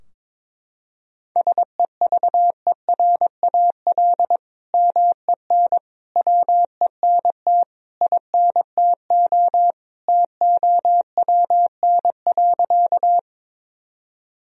Morse code, heard as a keyed tone, about 22 wpm, 700 Hz. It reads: SEVERAL MEN WENT INTO TOWN.
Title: Sentences from Top 1000 Words - Review Part 2 - 22wpm